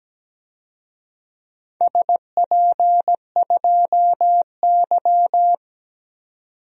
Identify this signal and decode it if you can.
Morse code, heard as a keyed tone, about 17 wpm, 700 Hz. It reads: SP2Y
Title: DXCC-100 Call Signs - 4 Characters - 17wpm